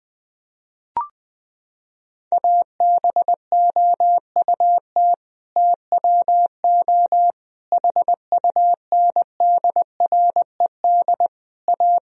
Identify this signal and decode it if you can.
Morse code, heard as a keyed tone, about 21 wpm, 700 Hz. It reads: ABOUT TWO HUNDRED A
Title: Sentences from Top 1000 Words - Review Part 2 - 20wpm